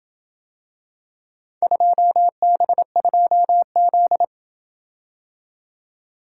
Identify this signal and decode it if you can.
Morse code, heard as a keyed tone, about 27 wpm, 700 Hz. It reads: 2B2Z